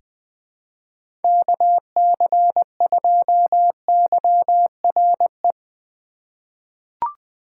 Morse code, heard as a keyed tone, about 20 wpm, 700 Hz. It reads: KC2YRE
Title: USA Call Signs - 6 Characters - 2x Character Spacing - 20wpm